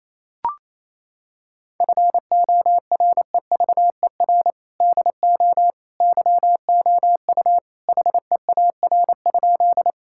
Morse code, heard as a keyed tone, about 28 wpm, 700 Hz. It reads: FOREVER DO YOU HEAR?
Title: Sentences from Top 1600 Words 28wpm